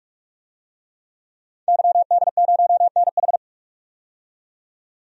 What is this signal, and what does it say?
Morse code, heard as a keyed tone, about 41 wpm, 700 Hz. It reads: YD0NH
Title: DXCC-100 Call Signs - 5 Characters - 45wpm